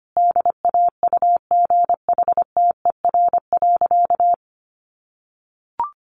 Morse code, heard as a keyed tone, about 25 wpm, 700 Hz. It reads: DAUGHTER.
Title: Sentences from Top 1500 Words 25wpm